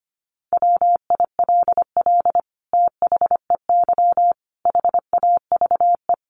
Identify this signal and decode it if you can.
Morse code, heard as a keyed tone, about 25 wpm, 700 Hz. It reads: WILL THEY HAVE
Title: Sentences from Top 1500 Words 25wpm